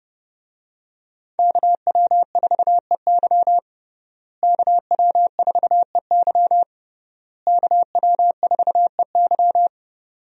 Morse code, heard as a keyed tone, about 30 wpm, 700 Hz. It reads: KW4EY KW4EY KW4EY
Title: USA Call Signs - 5 Characters - Repeated 3x - 30wpm